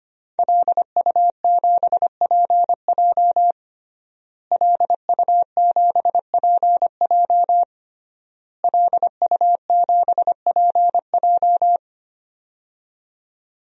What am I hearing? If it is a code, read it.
Morse code, heard as a keyed tone, about 25 wpm, 700 Hz. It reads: LU7PJ LU7PJ LU7PJ
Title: DXCC-100 Call Signs - 5 Characters - Repeated 3x - 25wpm